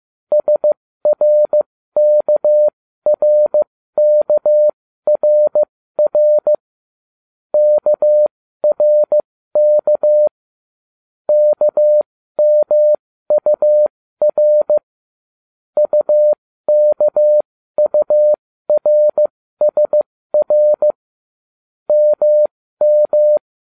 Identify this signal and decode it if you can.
Morse code, heard as a keyed tone, about 15 wpm, spaced as for 13 wpm, 600 Hz. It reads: SRKRKRR KRK KMUR UKURSR MM